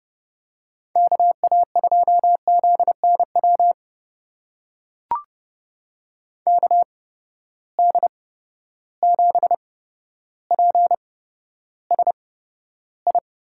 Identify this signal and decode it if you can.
Morse code, heard as a keyed tone, about 30 wpm, spaced as for 13 wpm, 700 Hz. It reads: KA2ZNW KD7PSI
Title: USA Call Signs - 6 Characters - 8x Character Spacing - 30wpm